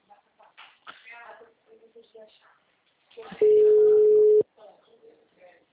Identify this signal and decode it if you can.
Ringback tone (Europe, 425 Hz), with speech.